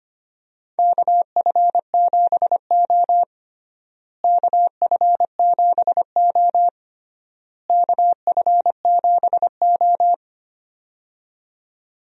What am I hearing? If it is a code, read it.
Morse code, heard as a keyed tone, about 25 wpm, 700 Hz. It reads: KF7O KF7O KF7O
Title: USA Call Signs - 4 Characters - Repeated 3x - 25wpm